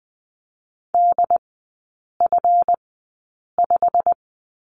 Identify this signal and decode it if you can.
Morse code, heard as a keyed tone, about 20 wpm, spaced as for 8 wpm, 690 Hz. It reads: DF5
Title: 145 Lesson 24e - Letter y - Three Characters 20wpm